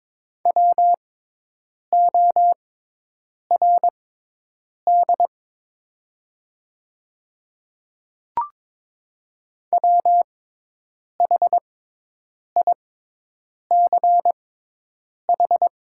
Morse code, heard as a keyed tone, about 23 wpm, spaced as for 8 wpm, 700 Hz. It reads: WORD WHICH
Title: 6x Character Spacing - Top 200 Words - 22wpm